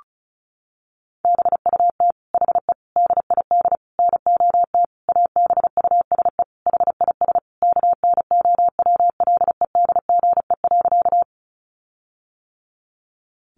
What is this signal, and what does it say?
Morse code, heard as a keyed tone, about 34 wpm, 700 Hz. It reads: BUT HE DID NOT ABUSE HIS KNOWLEDGE.
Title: Sentences from Top 1300 Words 35wpm